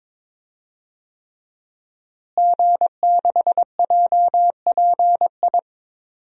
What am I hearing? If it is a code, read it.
Morse code, heard as a keyed tone, about 22 wpm, 700 Hz. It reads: G6JPI
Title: DXCC-100 Call Signs - 5 Characters - Repeated 3x - 22wpm